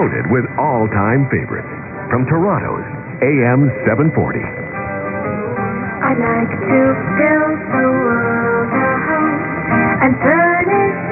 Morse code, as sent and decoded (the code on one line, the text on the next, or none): none